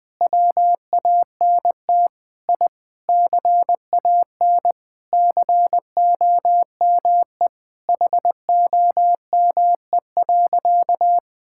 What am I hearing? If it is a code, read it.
Morse code, heard as a keyed tone, about 21 wpm, 700 Hz. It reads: WANT I CAN COME HOME.
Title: Sentences with 3-Word Phrases - 20wpm